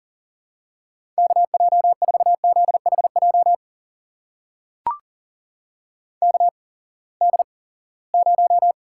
Morse code, heard as a keyed tone, about 37 wpm, spaced as for 22 wpm, 700 Hz. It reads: KJ4ZHJ KD0
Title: USA Call Signs - 6 Characters - 8x Character Spacing - 40wpm